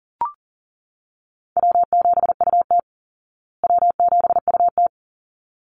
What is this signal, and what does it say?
Morse code, heard as a keyed tone, about 37 wpm, 700 Hz. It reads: W7UT W7UT
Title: Call Signs Repeated 3x - 40wpm